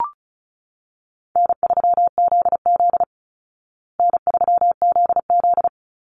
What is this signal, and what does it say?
Morse code, heard as a keyed tone, about 34 wpm, 700 Hz. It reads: N3ZZ N3ZZ
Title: Call Signs Repeated 3x - 35wpm